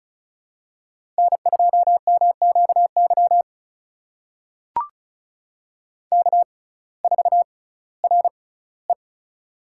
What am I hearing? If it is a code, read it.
Morse code, heard as a keyed tone, about 36 wpm, spaced as for 19 wpm, 700 Hz. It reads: N2MQY K4RE
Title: USA Call Signs - 6x Character Spacing - 35wpm